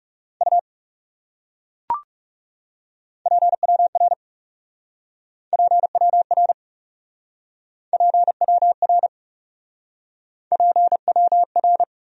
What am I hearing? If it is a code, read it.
Morse code, heard as a keyed tone, about 37 wpm, 700 Hz. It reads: U PWR PWR PWR PWR
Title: Speed-Racing Abbreviations - 30wpm